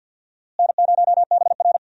Morse code, heard as a keyed tone, about 52 wpm, 700 Hz. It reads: N0BR
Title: USA Call Signs - 4 Characters - Repeated 3x - 50wpm